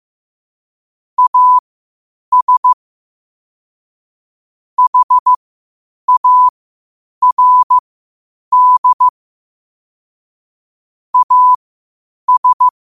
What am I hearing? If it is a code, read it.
Morse code, heard as a keyed tone, about 15 wpm, spaced as for 8 wpm, 1000 Hz. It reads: AS HARD AS
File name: qotdfiles / 8wpm